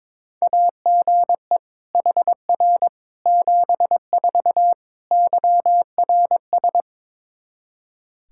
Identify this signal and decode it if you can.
Morse code, heard as a keyed tone, about 22 wpm, 700 Hz. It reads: AGE HR 74 YRS
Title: Practice QSO - Exchange 3 - More Info - 22wpm